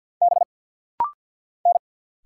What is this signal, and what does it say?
Morse code, heard as a keyed tone, about 50 wpm, 700 Hz. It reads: B N
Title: Single Letter-Number Alt - Rapid-Fire 50wpm